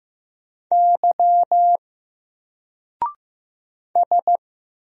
Morse code, heard as a keyed tone, about 15 wpm, 700 Hz. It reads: Y S
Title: Single Letter-Number 15wpm